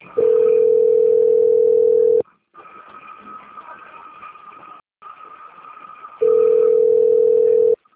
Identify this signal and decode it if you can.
Ringback tone (North America), 440 + 480 Hz.
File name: out- - - - - .53290